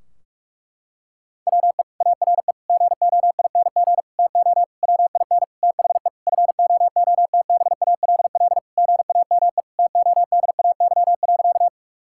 Morse code, heard as a keyed tone, about 46 wpm, 700 Hz. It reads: WE ARE GOING TO WIN THE FOOTBALL GAME TODAY.
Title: Sentences from Top 1600 Words 45wpm